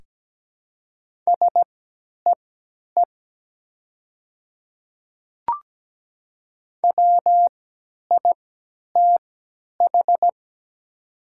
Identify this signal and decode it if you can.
Morse code, heard as a keyed tone, about 17 wpm, spaced as for 10 wpm, 700 Hz. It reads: SEE WITH
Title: 3x Character Spacing - Top 100 Words - 17wpm